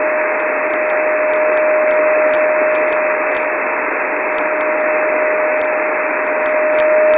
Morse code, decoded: none